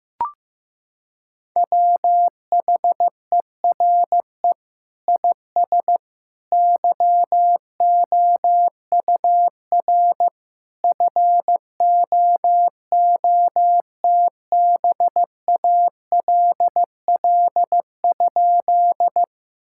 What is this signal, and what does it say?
Morse code, heard as a keyed tone, about 15 wpm, 700 Hz. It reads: WHERE IS YOUR FOOTBALL?
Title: Sentences from Top 1600 Words 15wpm